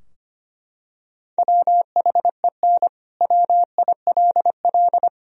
Morse code, heard as a keyed tone, about 25 wpm, 700 Hz. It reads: WHEN WILL
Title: Sentences from Top 1000 Words - Review Part 4 - 25wpm